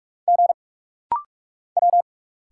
Morse code, heard as a keyed tone, about 44 wpm, 700 Hz. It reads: G W
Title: Single Letter-Number Alt - Rapid-Fire 45wpm